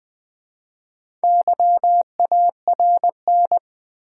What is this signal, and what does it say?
Morse code, heard as a keyed tone, about 20 wpm, 700 Hz. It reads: YARN
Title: Easy Words with Hints - 20wpm